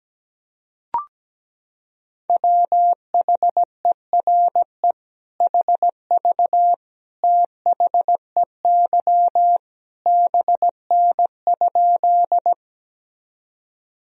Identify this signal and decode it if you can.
Morse code, heard as a keyed tone, about 17 wpm, 700 Hz. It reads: WHERE HV THEY BN?
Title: Sentences from Top 200 Words and Abbreviations 17wpm